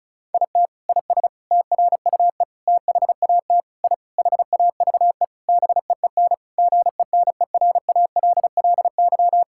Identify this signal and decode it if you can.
Morse code, heard as a keyed tone, about 34 wpm, 700 Hz. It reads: IT IS TRUE THAT I HAVE BEEN GENERALLY
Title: Sentences from Top 1000 Words - Review Part 6 - 35wpm